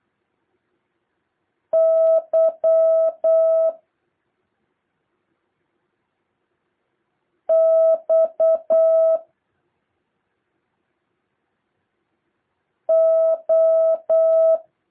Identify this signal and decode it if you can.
Morse code, heard as a keyed tone, about 8 wpm, 650 Hz. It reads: Y X O
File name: jh8cbh / 2023cw